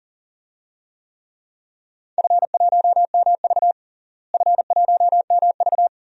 Morse code, heard as a keyed tone, about 40 wpm, 700 Hz. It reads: F1MV F1MV
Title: DXCC-100 Call Signs - 4 Characters - Repeated 2x - 40wpm